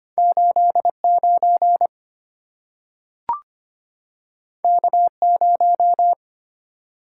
Morse code, heard as a keyed tone, about 25 wpm, 700 Hz. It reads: 89 K0
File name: Two Letter-Number Combinations - 25wpm